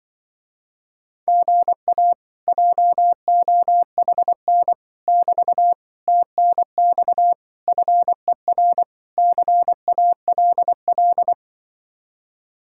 Morse code, heard as a keyed tone, about 24 wpm, 700 Hz. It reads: GA JOHN <BT> TNX FER CALL
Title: Speed-Racing Common QSO Phrases - 20wpm